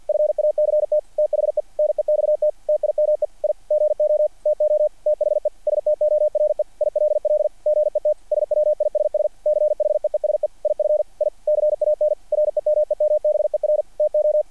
Morse code, heard as a keyed tone, about 48 wpm, 660 Hz. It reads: FACT THE NEXT TIME I GO TO THE STORE ILL GET SWISS CHEESE IF I CAN REMEMBER TO
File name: archived / CRSnet